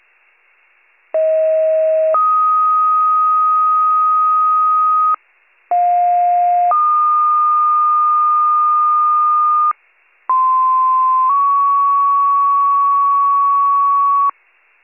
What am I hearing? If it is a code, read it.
Two-tone page, heard as a one-second tone, then a three-second tone, about 640 then 1250 Hz, about 710 then 1220 Hz, about 1040 then 1120 Hz. It is heard from 1.1 to 14.3 s.